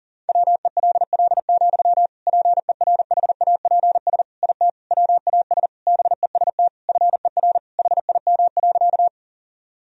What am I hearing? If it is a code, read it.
Morse code, heard as a keyed tone, about 37 wpm, 700 Hz. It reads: WELL, PERHAPS IT WAS BEST FER HIM.
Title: Sentences from Top 500 Words and Abbreviations 40wpm